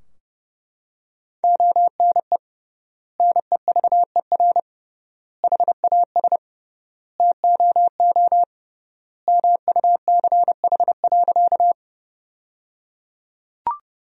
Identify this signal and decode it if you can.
Morse code, heard as a keyed tone, about 30 wpm, 700 Hz. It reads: ONE NEVER HAS TOO MUCH.
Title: 3x Word-Spacing - Sentences from Top 300 Words - 30wpm